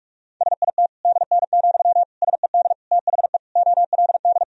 Morse code, heard as a keyed tone, about 46 wpm, 690 Hz. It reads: SIT DN, SED THE OLD
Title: Sentences from Top 500 Words and Abbreviations 45wpm